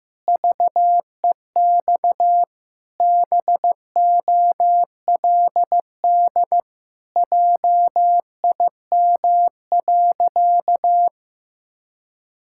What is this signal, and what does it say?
Morse code, heard as a keyed tone, about 16 wpm, 700 Hz. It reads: VEX BOLD JIM.
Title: Pangram Challenge v2 - 15wpm